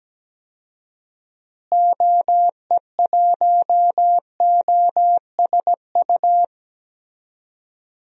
Morse code, heard as a keyed tone, about 17 wpm, 700 Hz. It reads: OE1OSU
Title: DXCC-100 Call Signs - 6 Characters - Repeated 2x - 17wpm